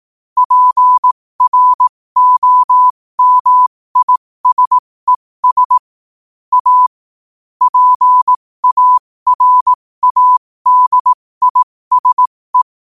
Morse code, heard as a keyed tone, about 18 wpm, spaced as for 15 wpm, 1000 Hz. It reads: PROMISES A PARADISE
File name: qotdfiles / 15wpm